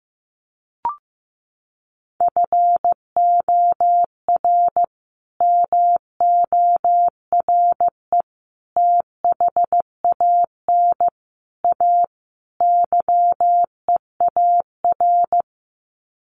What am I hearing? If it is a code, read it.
Morse code, heard as a keyed tone, about 15 wpm, 700 Hz. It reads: FOR MORE THAN A YEAR
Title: Sets of 5 Words 15wpm